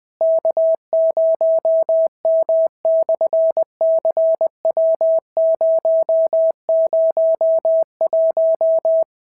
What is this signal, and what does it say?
Morse code, heard as a keyed tone, about 20 wpm, 650 Hz. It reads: K0M/CW001
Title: SOTA QSO Practice 20wpm